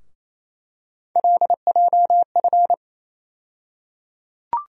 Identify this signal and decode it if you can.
Morse code, heard as a keyed tone, about 28 wpm, 700 Hz. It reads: LJF
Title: USA Call Sign Postfixes - 3 Letters 28wpm